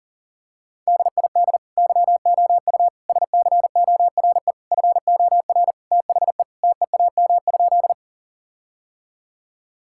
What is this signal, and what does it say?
Morse code, heard as a keyed tone, about 40 wpm, 700 Hz. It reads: DID YOU SCORE FOR THE TEAM?
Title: Sentences from Top 1600 Words 40wpm